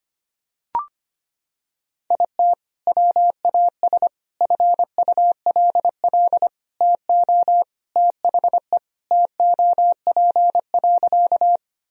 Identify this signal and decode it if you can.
Morse code, heard as a keyed tone, about 25 wpm, 700 Hz. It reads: IT WAS FULL TO THE TOP.
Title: Sentences from Top 1000 Words - Review Part 3 - 25wpm